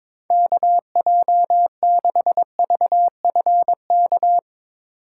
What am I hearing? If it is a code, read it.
Morse code, heard as a keyed tone, about 22 wpm, 700 Hz. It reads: KJ6VFK
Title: USA Call Signs - 6 Characters - 4x Character Spacing - 22wpm